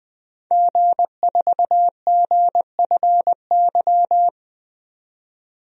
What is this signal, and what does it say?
Morse code, heard as a keyed tone, about 20 wpm, 700 Hz. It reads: G4GFY
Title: DXCC-100 Call Signs - 5 Characters - 20wpm